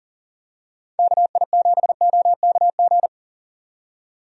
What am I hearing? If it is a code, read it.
Morse code, heard as a keyed tone, about 40 wpm, 700 Hz. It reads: KI7OKG
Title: USA Call Signs - 6 Characters - 40wpm